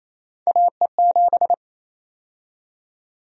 Morse code, heard as a keyed tone, about 28 wpm, 700 Hz. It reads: AE7
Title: USA Call Sign Prefixes - 28wpm